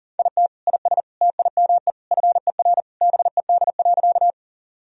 Morse code, heard as a keyed tone, about 40 wpm, 700 Hz. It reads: IT IS TIME FER BED.
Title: Sentences from Top 500 Words and Abbreviations 40wpm